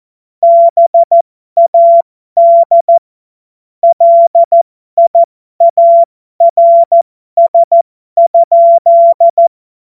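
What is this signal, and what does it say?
Morse code, heard as a keyed tone, about 14 wpm, spaced as for 12 wpm, 680 Hz. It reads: BAD LIARS?
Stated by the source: Morse joke practice at effective 60, Farnsworth 70.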